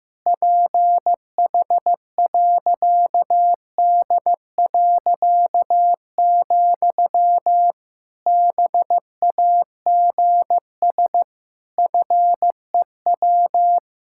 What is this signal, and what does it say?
Morse code, heard as a keyed tone, about 16 wpm, 700 Hz. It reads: PH.D., BAGS FEW
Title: Pangram Challenge v2 - 15wpm